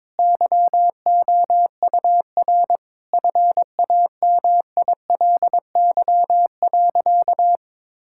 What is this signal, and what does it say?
Morse code, heard as a keyed tone, about 22 wpm, 700 Hz. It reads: YOUR FAMILY.
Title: Sentences from Top 1600 Words 22wpm